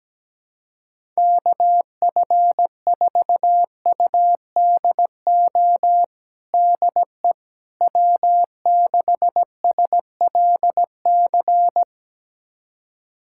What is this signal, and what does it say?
Morse code, heard as a keyed tone, about 17 wpm, 700 Hz. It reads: KF4UDO DE W6SLC
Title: USA Callsign DE Callsign 17wpm